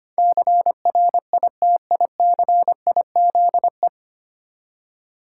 Morse code, heard as a keyed tone, about 25 wpm, 700 Hz. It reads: CRITICIZE
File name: Nine Letter Words - 25wpm